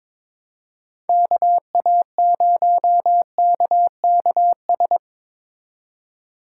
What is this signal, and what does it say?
Morse code, heard as a keyed tone, about 22 wpm, 700 Hz. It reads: KA0KKS
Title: USA Call Signs - 8x Character Spacing - 22wpm